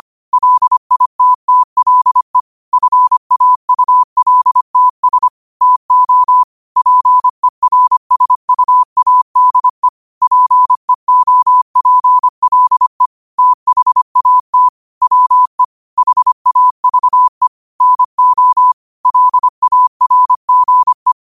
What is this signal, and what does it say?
Morse code, heard as a keyed tone, about 25 wpm, 1000 Hz. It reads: LITTLE FAULTS TO PERSUADE PEOPLE THAT WE HAVE NO LARGE